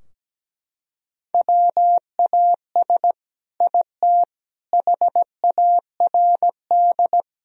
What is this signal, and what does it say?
Morse code, heard as a keyed tone, about 17 wpm, 700 Hz. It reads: WAS IT HARD
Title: Sentences from Top 1000 Words - Review Part 4 - 17wpm